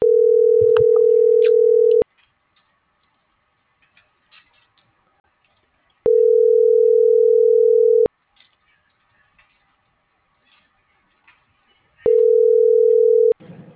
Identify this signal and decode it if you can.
Ringback tone (North America), 440 + 480 Hz.
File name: CENTRAL SOTOVA / out alejandro.rodriguez - .2217